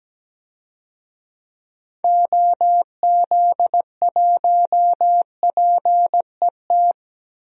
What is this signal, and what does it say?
Morse code, heard as a keyed tone, about 17 wpm, 700 Hz. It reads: OZ1PET
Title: DXCC-100 Call Signs - 6 Characters - 17wpm